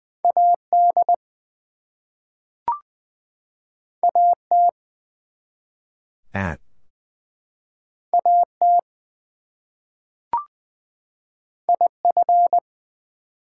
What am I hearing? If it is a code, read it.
Morse code, heard as a keyed tone, about 20 wpm, 700 Hz. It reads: AD AT AT IF
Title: Two Letter Words - 20wpm